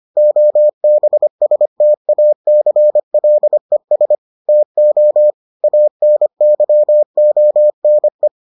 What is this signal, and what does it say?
Morse code, heard as a keyed tone, about 25 wpm, 600 Hz. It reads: OBSTACLES TO ANYONE